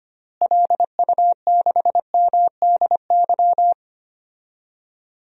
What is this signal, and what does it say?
Morse code, heard as a keyed tone, about 25 wpm, 700 Hz. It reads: LU6MDY
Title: DXCC-100 Call Signs - 6 Characters - Repeated 3x - 25wpm